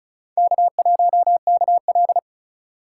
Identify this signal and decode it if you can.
Morse code, heard as a keyed tone, about 45 wpm, 700 Hz. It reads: K1KL